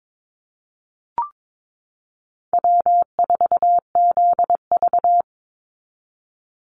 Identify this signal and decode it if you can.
Morse code, heard as a keyed tone, about 22 wpm, 700 Hz. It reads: W4ZV
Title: NA SOTA Chaser Callsigns 22wpm